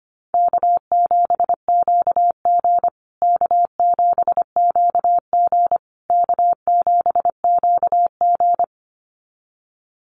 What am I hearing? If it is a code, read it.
Morse code, heard as a keyed tone, about 25 wpm, 700 Hz. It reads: K7QG K7QG K7QG
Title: USA Call Signs Repeated 3x 25wpm